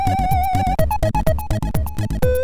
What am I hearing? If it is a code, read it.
There is no code.